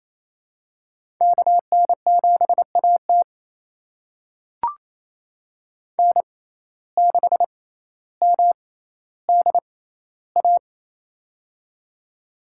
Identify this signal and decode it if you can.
Morse code, heard as a keyed tone, about 28 wpm, spaced as for 14 wpm, 700 Hz. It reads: KN7AT N6MDA
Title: USA Call Signs - 5 Characters - 6x Character Spacing - 28wpm